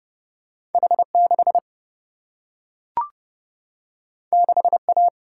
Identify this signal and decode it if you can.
Morse code, heard as a keyed tone, about 30 wpm, 700 Hz. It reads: H6 6A
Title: Single-Letter Single-Number 30wpm